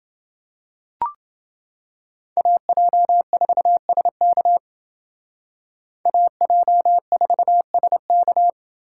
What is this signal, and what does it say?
Morse code, heard as a keyed tone, about 29 wpm, 700 Hz. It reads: AJ4SK AJ4SK